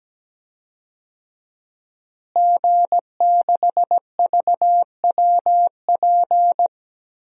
Morse code, heard as a keyed tone, about 17 wpm, 700 Hz. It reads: G6VWP
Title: DXCC-100 Call Signs - 5 Characters - Repeated 2x - 17wpm